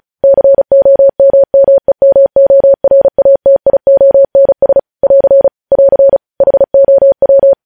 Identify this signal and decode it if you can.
Morse code, heard as a keyed tone, about 36 wpm, 550 Hz. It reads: COMMEMORATIONS <AR> <AR> HOW
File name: podcast / News headlines-1x-35x35